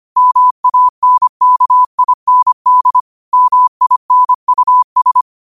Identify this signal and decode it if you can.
Morse code, heard as a keyed tone, about 25 wpm, 1000 Hz. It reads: MANKIND MINUS